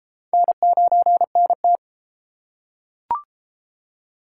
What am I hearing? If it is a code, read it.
Morse code, heard as a keyed tone, about 32 wpm, 700 Hz. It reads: N9NT